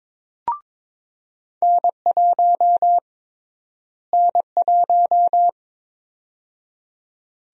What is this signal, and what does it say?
Morse code, heard as a keyed tone, about 22 wpm, 700 Hz. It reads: N1 N1
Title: USA Call Sign Prefixes - Repeated 2x - 22wpm